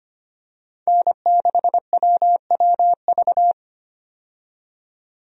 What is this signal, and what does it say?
Morse code, heard as a keyed tone, about 25 wpm, 700 Hz. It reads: N6WWV
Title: USA Call Signs - 5 Characters - 25wpm